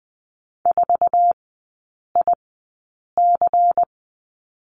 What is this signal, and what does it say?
Morse code, heard as a keyed tone, about 20 wpm, spaced as for 8 wpm, 700 Hz. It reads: 4IC
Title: 093 Lesson 16e - Letter c - Three Characters 20wpm